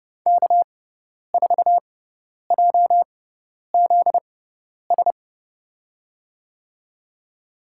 Morse code, heard as a keyed tone, about 30 wpm, spaced as for 10 wpm, 700 Hz. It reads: K4JZS
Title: USA Call Signs - 5 Characters - 6x Character Spacing - 30wpm